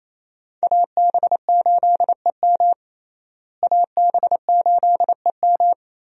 Morse code, heard as a keyed tone, about 28 wpm, 700 Hz. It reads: AB8EM AB8EM
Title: USA Call Signs - 5 Characters - Repeated 2x - 28wpm